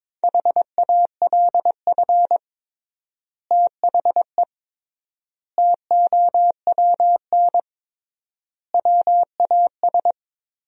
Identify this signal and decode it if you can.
Morse code, heard as a keyed tone, about 22 wpm, 700 Hz. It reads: HALF THE TOWN WAS
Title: 3x Word-Spacing - Sentences from Top 500 Words - 22wpm